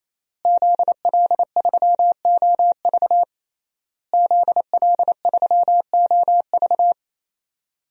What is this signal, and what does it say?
Morse code, heard as a keyed tone, about 28 wpm, 700 Hz. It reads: ZL3OV ZL3OV
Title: DXCC-100 Call Signs - 5 Characters - Repeated 3x - 28wpm